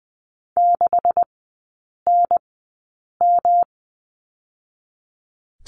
Morse code, heard as a keyed tone, about 20 wpm, spaced as for 8 wpm, 700 Hz. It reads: 6NM
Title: 127 Lesson 21e - Number 6 - Three Characters 20wpm